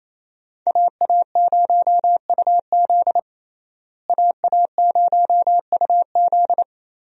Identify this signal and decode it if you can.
Morse code, heard as a keyed tone, about 28 wpm, 700 Hz. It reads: AA0UZ AA0UZ
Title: USA Call Signs - 5 Characters - Repeated 2x - 28wpm